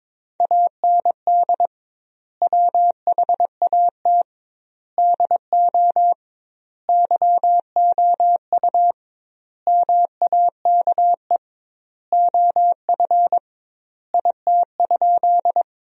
Morse code, heard as a keyed tone, about 22 wpm, 700 Hz. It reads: AND WHAT DO YOU MAKE OF IT?
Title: 2x Word-Spacing - Sentences from Top 100 Words - 22wpm